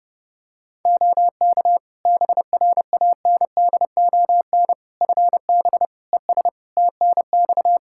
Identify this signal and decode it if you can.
Morse code, heard as a keyed tone, about 32 wpm, 700 Hz. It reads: OK BRANDON FB ES TNX
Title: Practice QSO - Exchange 2 - Info - 30wpm